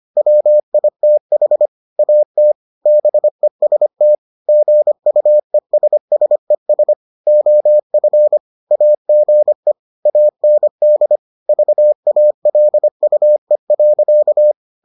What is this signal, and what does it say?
Morse code, heard as a keyed tone, about 25 wpm, 600 Hz. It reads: WITH AT BEST GUESSES OF AGE AND VALUE.